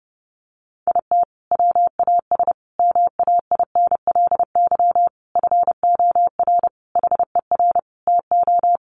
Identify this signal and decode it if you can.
Morse code, heard as a keyed tone, about 30 wpm, 700 Hz. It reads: IT WAS MAINLY FOR HER TO
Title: Sentences from Top 1400 Words 30wpm